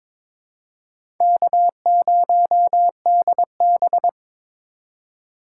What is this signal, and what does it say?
Morse code, heard as a keyed tone, about 22 wpm, 700 Hz. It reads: K0DB
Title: USA Call Signs - 2x Character Spacing - 22wpm